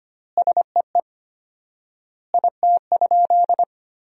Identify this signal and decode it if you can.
Morse code, heard as a keyed tone, about 25 wpm, 700 Hz. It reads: SEE IT?
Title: 4x Word-Spacing - Sentences from Top 100 Words - 25wpm